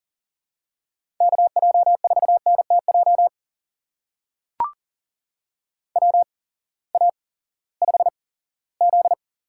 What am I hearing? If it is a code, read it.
Morse code, heard as a keyed tone, about 38 wpm, spaced as for 20 wpm, 700 Hz. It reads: KJ4NTJ WA5Z
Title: USA Call Signs - 6 Characters - 8x Character Spacing - 40wpm